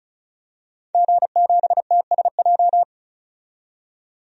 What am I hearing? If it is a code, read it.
Morse code, heard as a keyed tone, about 36 wpm, 700 Hz. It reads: G7TSJ
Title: DXCC-100 Call Signs - 5 Characters - 35wpm